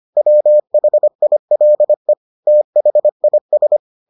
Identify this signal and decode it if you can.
Morse code, heard as a keyed tone, about 25 wpm, 600 Hz. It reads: WHILE THIS